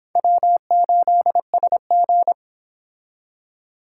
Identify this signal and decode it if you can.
Morse code, heard as a keyed tone, about 29 wpm, 700 Hz. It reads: W8SG